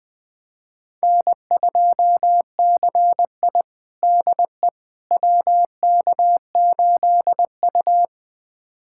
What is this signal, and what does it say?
Morse code, heard as a keyed tone, about 21 wpm, 700 Hz. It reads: N2CI DE WK8U
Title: USA Callsign DE Callsign - 4 Characters - 20wpm